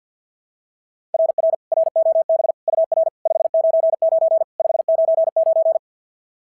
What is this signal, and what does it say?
Morse code, heard as a keyed tone, about 48 wpm, 690 Hz. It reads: RR ROB UR 599 599
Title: SOTA QSO Practice 50wpm